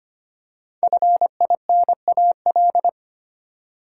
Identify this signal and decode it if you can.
Morse code, heard as a keyed tone, about 25 wpm, 700 Hz. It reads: FINAL
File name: 5 Letter Words - First 3 Letters Given - 25wpm